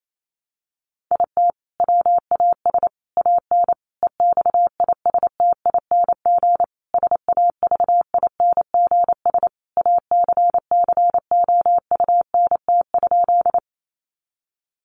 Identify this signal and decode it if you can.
Morse code, heard as a keyed tone, about 28 wpm, 700 Hz. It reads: IT WAS AN EXISTING SAVINGS ACCOUNT?
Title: Sentences from Top 1500 Words 28wpm